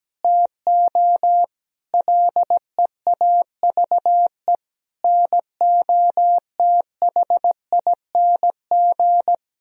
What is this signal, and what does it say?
Morse code, heard as a keyed tone, about 17 wpm, 700 Hz. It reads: TO LEAVE NOTHING
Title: Sentences from Top 1000 Words - Review Part 1 - 17wpm